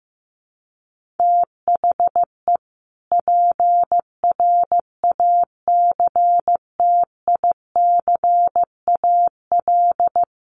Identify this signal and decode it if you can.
Morse code, heard as a keyed tone, about 15 wpm, 700 Hz. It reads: THE PRACTICAL
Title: Sentences from Top 1400 Words 15wpm